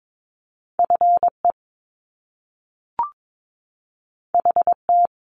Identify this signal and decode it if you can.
Morse code, heard as a keyed tone, about 22 wpm, 700 Hz. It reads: FE HT
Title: Top Two Letter Combinations in Words 22wpm